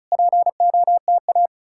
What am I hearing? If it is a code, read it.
Morse code, heard as a keyed tone, about 46 wpm, spaced as for 41 wpm, 700 Hz. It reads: POTA
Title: Speed-Racing Common QSO Phrases Encore - 35wpm